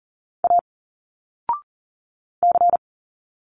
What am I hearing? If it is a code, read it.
Morse code, heard as a keyed tone, about 40 wpm, 700 Hz. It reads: A C
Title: Single Letters 40wpm